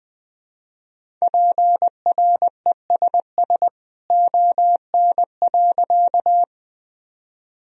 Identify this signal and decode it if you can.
Morse code, heard as a keyed tone, about 21 wpm, 700 Hz. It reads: PRESS ON.
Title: Sentences from Top 1000 Words - Review Part 1 - 20wpm